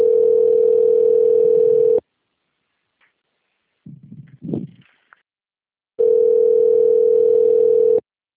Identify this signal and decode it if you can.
Ringback tone (North America), 440 + 480 Hz.